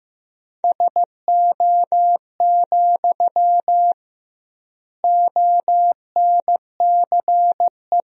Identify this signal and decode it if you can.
Morse code, heard as a keyed tone, about 16 wpm, 700 Hz. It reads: SO, ONCE
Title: 2x Word-Spacing - Sentences from Top 300 Words - 15wpm